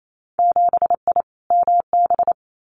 Morse code, heard as a keyed tone, about 28 wpm, 700 Hz. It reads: 7I MB
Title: Winter Field Day Exchanges Encore 28wpm